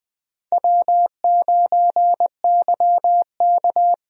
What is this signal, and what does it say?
Morse code, heard as a keyed tone, about 20 wpm, 700 Hz. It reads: W9YK
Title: USA Call Signs - 4 Characters - Repeated 3x - 20wpm